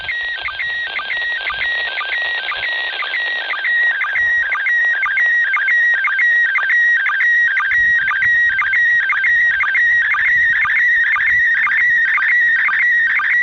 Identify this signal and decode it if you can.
SSTV (slow-scan television), PD 120.
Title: ISS SSTV